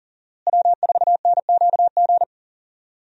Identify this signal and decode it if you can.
Morse code, heard as a keyed tone, about 40 wpm, 700 Hz. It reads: W4NQG
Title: USA Call Signs - 5 Characters - Repeated 2x - 40wpm